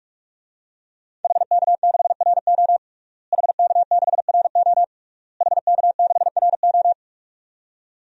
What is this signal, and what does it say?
Morse code, heard as a keyed tone, about 42 wpm, 700 Hz. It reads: HK6RO HK6RO HK6RO
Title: DXCC-100 Call Signs - 5 Characters - Repeated 3x - 45wpm